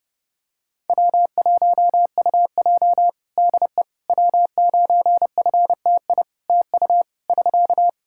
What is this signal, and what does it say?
Morse code, heard as a keyed tone, about 32 wpm, 700 Hz. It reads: W1UJ DE W9FTI TU <SK>
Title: Practice QSO - Exchange 4 - Ending - 30wpm